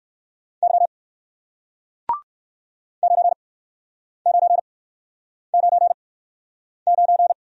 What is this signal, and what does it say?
Morse code, heard as a keyed tone, about 61 wpm, 700 Hz. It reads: Y 9 9 9 9